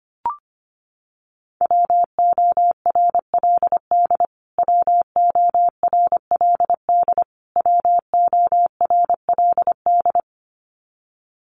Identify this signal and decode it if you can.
Morse code, heard as a keyed tone, about 25 wpm, 700 Hz. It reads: WORLD WORLD WORLD
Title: Top 500 Words Repeated 25wpm